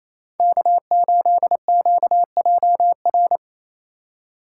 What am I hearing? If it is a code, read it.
Morse code, heard as a keyed tone, about 28 wpm, 700 Hz. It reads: K8QJR
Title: USA Call Signs - 5 Characters - 2x Character Spacing - 28wpm